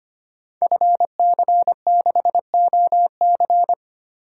Morse code, heard as a keyed tone, about 25 wpm, 700 Hz. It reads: FC6OC
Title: Code Groups - 5 Letter:Numbers - 25wpm